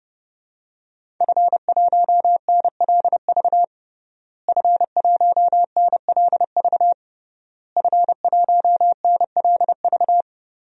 Morse code, heard as a keyed tone, about 32 wpm, 700 Hz. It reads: F1NLV F1NLV F1NLV
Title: DXCC-100 Call Signs - 5 Characters - Repeated 3x - 30wpm